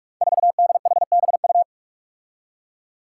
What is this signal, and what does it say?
Morse code, heard as a keyed tone, about 44 wpm, 700 Hz. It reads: 4DHBU
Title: Five Challenging Letter-Number Combinations 45wpm